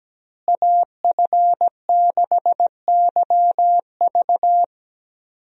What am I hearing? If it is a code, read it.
Morse code, heard as a keyed tone, about 17 wpm, 700 Hz. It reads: AF6YV
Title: USA Call Signs - 5 Characters - Repeated 2x - 17wpm